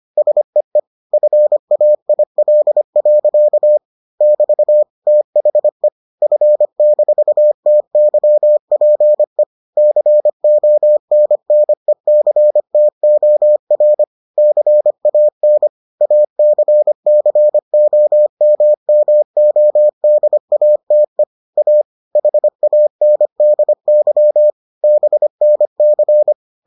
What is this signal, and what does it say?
Morse code, heard as a keyed tone, about 25 wpm, 600 Hz. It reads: SEE FAIL. <BT> THE F-TYPE CONNECTOR CAN ACCOMMODATE A HANDY BNC